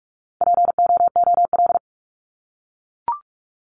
Morse code, heard as a keyed tone, about 44 wpm, 700 Hz. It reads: POOL
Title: Top 1751-2000 Words 45wpm